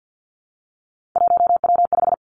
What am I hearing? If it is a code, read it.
Morse code, heard as a keyed tone, about 50 wpm, 700 Hz. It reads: 1W5
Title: Three Challenging Letter-Number Combinations 50wpm